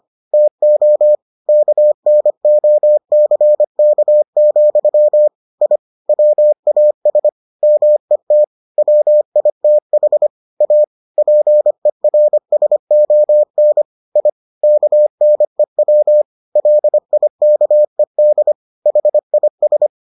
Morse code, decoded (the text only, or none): TO KNOCK, I WAS MET WITH A PERSON I KNEW LIKED HIS